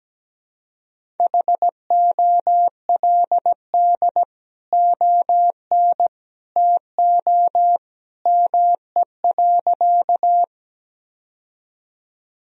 Morse code, heard as a keyed tone, about 17 wpm, 700 Hz. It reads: HOLD ON TO ME.
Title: Sentences from Top 1000 Words - Review Part 2 - 17wpm